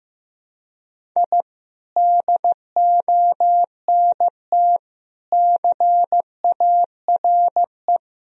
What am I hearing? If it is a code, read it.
Morse code, heard as a keyed tone, about 15 wpm, 700 Hz. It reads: I DONT CARE
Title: Sentences from Top 1000 Words - Review Part 1 - 15wpm